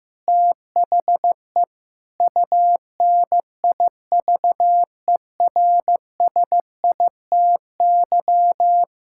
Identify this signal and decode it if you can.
Morse code, heard as a keyed tone, about 15 wpm, 700 Hz. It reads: THE UNIVERSITY
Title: Sentences from Top 1000 Words - Review Part 5 - 15wpm